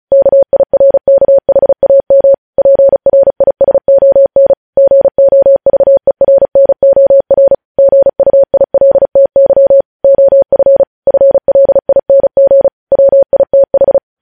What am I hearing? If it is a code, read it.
Morse code, heard as a keyed tone, about 36 wpm, 550 Hz. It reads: KIRKHAM PRISON GOVERNOR GUILTY OF FLING WITH
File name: podcast / News headlines-1x-35x35